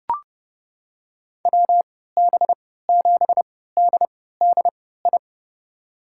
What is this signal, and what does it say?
Morse code, heard as a keyed tone, about 32 wpm, spaced as for 17 wpm, 700 Hz. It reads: WB7DDI
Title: USA Call Signs - 6 Characters - 3x Character Spacing - 30wpm